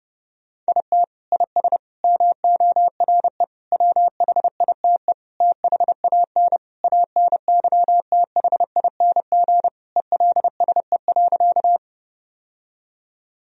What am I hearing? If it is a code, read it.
Morse code, heard as a keyed tone, about 30 wpm, 700 Hz. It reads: IT IS MORE WHITE THAN ANYTHING ELSE.
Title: Sentences from Top 1000 Words - Review Part 1 - 30wpm